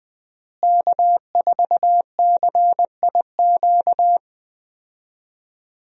Morse code, heard as a keyed tone, about 20 wpm, 700 Hz. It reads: K4CIQ
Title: USA Call Signs - 5 Characters - Repeated 3x - 20wpm